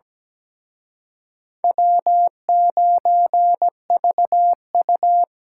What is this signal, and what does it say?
Morse code, heard as a keyed tone, about 20 wpm, 700 Hz. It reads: W9VU
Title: Speed Racing - USA 1x2 Call Signs 17wpm